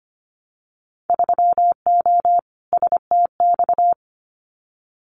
Morse code, heard as a keyed tone, about 25 wpm, 700 Hz. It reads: 3O STX
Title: Winter Field Day Exchanges 25wpm